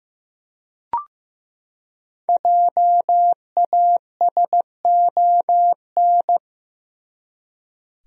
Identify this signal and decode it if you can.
Morse code, heard as a keyed tone, about 15 wpm, 700 Hz. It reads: JASON
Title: USA Common Names - 15wpm